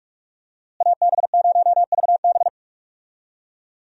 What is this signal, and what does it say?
Morse code, heard as a keyed tone, about 46 wpm, 700 Hz. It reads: AB0VB
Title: USA Call Signs - 5 Characters - Repeated 2x - 45wpm